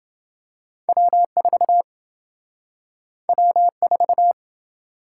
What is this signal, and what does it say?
Morse code, heard as a keyed tone, about 26 wpm, 700 Hz. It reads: W4 W4